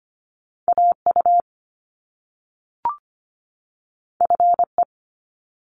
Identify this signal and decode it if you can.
Morse code, heard as a keyed tone, about 25 wpm, 700 Hz. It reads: AU FE
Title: Top Two Letter Combinations in Words 25wpm